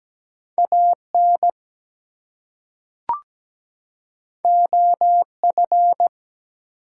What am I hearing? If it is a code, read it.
Morse code, heard as a keyed tone, about 17 wpm, 700 Hz. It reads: AN OF
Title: Two Letter Words - 17wpm